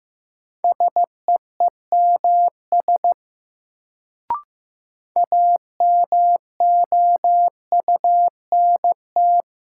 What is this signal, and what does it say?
Morse code, heard as a keyed tone, about 15 wpm, 700 Hz. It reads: SEEMS AMOUNT
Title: Sets of 3 Words - New Word Drill - 15wpm